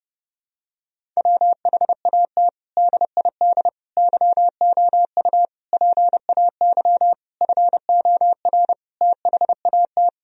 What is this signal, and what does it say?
Morse code, heard as a keyed tone, about 30 wpm, 700 Hz. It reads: WHAT DID YOU PAY FOR THAT
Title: Sentences from Top 1000 Words - Review Part 3 - 30wpm